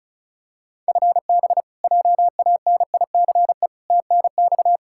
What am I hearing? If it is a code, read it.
Morse code, heard as a keyed tone, about 36 wpm, 700 Hz. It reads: FB JANICE TNX
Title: K1USN SST QSO Practice 35wpm